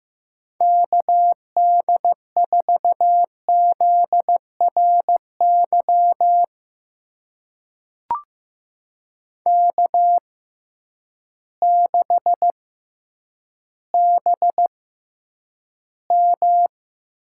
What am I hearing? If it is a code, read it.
Morse code, heard as a keyed tone, about 15 wpm, spaced as for 9 wpm, 700 Hz. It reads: KD4ZRY K6BM
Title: USA Call Signs - 6x Character Spacing - 15wpm